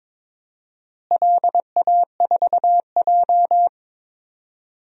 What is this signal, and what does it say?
Morse code, heard as a keyed tone, about 22 wpm, 700 Hz. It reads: LA4J
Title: DXCC-100 Call Signs - 4 Characters - Repeated 2x - 22wpm